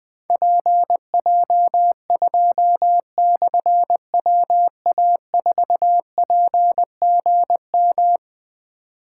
Morse code, heard as a keyed tone, about 20 wpm, 700 Hz. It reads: PJ2/WA4PGM
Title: Difficult Call Signs 20wpm